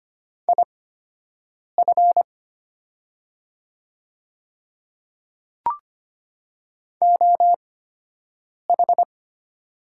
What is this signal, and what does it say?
Morse code, heard as a keyed tone, about 25 wpm, spaced as for 7 wpm, 690 Hz. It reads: IF OH
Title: 8x Character Spacing - Top 500 Words - 25wpm